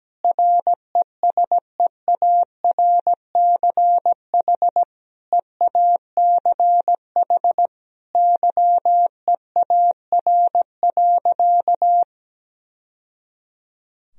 Morse code, heard as a keyed tone, about 17 wpm, 700 Hz. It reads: RESEARCH EACH YEAR.
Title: Sentences from Top 1000 Words - Review Part 4 - 17wpm